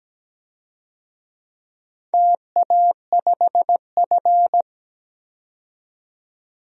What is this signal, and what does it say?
Morse code, heard as a keyed tone, about 17 wpm, 700 Hz. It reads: TA5F
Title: DXCC-100 Call Signs - 4 Characters - 17wpm